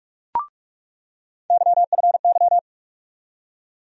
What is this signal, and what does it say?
Morse code, heard as a keyed tone, about 47 wpm, 700 Hz. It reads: YFY
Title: USA Call Sign Postfixes - 3 Letters 45wpm